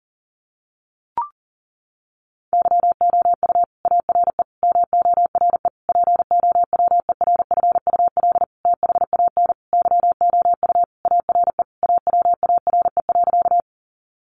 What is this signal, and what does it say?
Morse code, heard as a keyed tone, about 40 wpm, 700 Hz. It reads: YOU ARE MORE POWERFUL THAN YOU ARE AWARE.
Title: Sentences from Top 1300 Words 40wpm